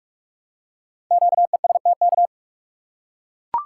Morse code, heard as a keyed tone, about 44 wpm, 700 Hz. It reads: QESTK
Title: Code Groups - 5 Letters - 45wpm